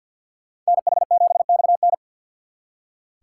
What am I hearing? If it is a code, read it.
Morse code, heard as a keyed tone, about 63 wpm, 700 Hz. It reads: NH7XN